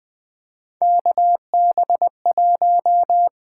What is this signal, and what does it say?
Morse code, heard as a keyed tone, about 20 wpm, 700 Hz. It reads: KB1
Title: USA Call Signs - 6 Characters - 8x Character Spacing - 20wpm